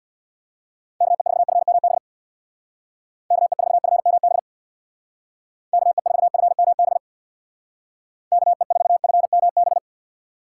Morse code, heard as a keyed tone, about 60 wpm, 700 Hz. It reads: KE4FMB KE4FMB KE4FMB KE4FMB
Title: Speed Racing - USA 2x3 Call Signs 50wpm